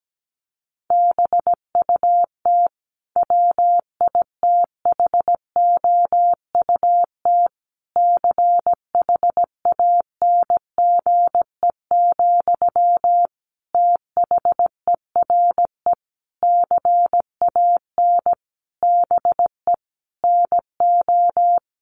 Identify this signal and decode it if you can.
Morse code, heard as a keyed tone, about 17 wpm, 700 Hz. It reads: BUT WITHOUT CHANGE, THERE CAN BE NO
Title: Sentences from Top 1400 Words 17wpm